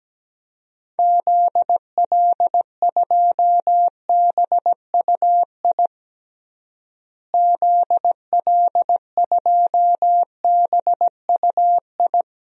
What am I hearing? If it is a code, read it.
Morse code, heard as a keyed tone, about 17 wpm, 700 Hz. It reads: ZL2BUI ZL2BUI
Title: DXCC-100 Call Signs - 6 Characters - Repeated 2x - 17wpm